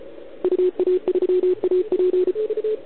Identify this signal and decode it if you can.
Morse code, heard as a keyed tone, about 33 wpm, 360 Hz. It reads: UA3AP
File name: ARCHIV / OKDX2020 / OL4Wa